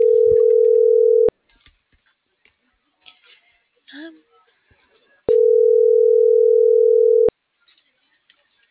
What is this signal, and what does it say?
Ringback tone (North America), 440 + 480 Hz, with speech.